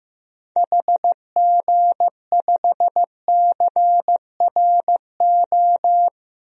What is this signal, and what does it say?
Morse code, heard as a keyed tone, about 15 wpm, 700 Hz. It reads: HG5CRO
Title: DXCC-100 Call Signs - 6 Characters - Repeated 3x - 15wpm